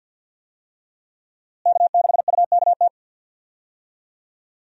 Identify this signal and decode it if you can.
Morse code, heard as a keyed tone, about 47 wpm, 700 Hz. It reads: K6UKT
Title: USA Call Signs - 5 Characters - Repeated 3x - 50wpm